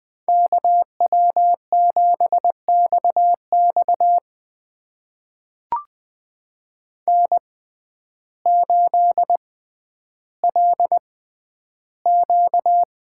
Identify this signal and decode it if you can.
Morse code, heard as a keyed tone, about 20 wpm, spaced as for 11 wpm, 700 Hz. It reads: KW7XX N8LQ
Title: USA Call Signs - 5 Characters - 6x Character Spacing - 20wpm